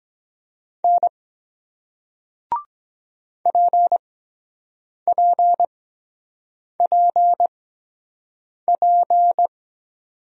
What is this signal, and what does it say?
Morse code, heard as a keyed tone, about 23 wpm, spaced as for 7 wpm, 700 Hz. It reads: N PPPP